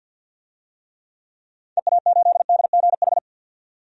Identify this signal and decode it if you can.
Morse code, heard as a keyed tone, about 51 wpm, 700 Hz. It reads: EA8DGH
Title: DXCC-100 Call Signs - 6 Characters - 50wpm